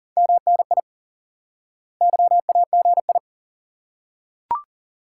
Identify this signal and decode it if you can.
Morse code, heard as a keyed tone, about 40 wpm, 700 Hz. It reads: MNI YAGI
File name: IQE Course 3 - Lesson 89 - YAGI - Two Elements - 40wpm